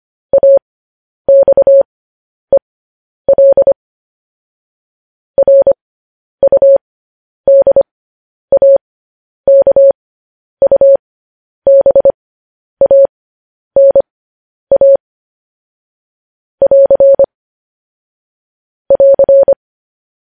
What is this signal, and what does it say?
Morse code, heard as a keyed tone, about 25 wpm, spaced as for 10 wpm, 550 Hz. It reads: AXEL RUDAKUBANA <AR> <AR>
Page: news-headlines-in-morse-code-at-10-wpm